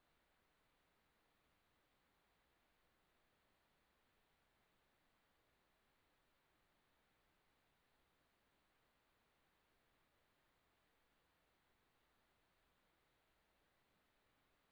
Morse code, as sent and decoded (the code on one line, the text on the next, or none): none